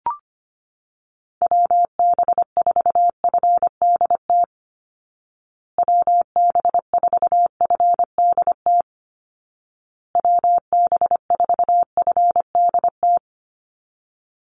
Morse code, heard as a keyed tone, about 25 wpm, 700 Hz. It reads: WB4FDT WB4FDT WB4FDT
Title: Call Signs Repeated 2x - 25wpm